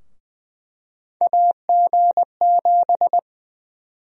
Morse code, heard as a keyed tone, about 20 wpm, 700 Hz. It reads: AG7
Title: USA Call Sign Prefixes - Repeated 2x - 20wpm